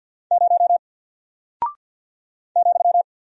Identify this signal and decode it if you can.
Morse code, heard as a keyed tone, about 48 wpm, 700 Hz. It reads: 0 ,
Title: Single Character (Letter:Number:Punctuation:Prosign) Alt 50wpm